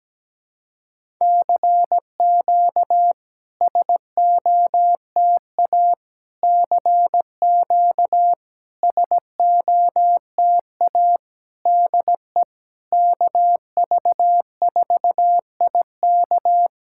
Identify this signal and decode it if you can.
Morse code, heard as a keyed tone, about 17 wpm, 700 Hz. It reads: CQ SOTA CQ SOTA DE KV4IK
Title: SOTA QSO Practice 17wpm